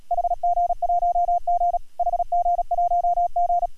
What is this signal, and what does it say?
Morse code, heard as a keyed tone, about 34 wpm, 690 Hz, with radio noise. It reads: HG1G HG1G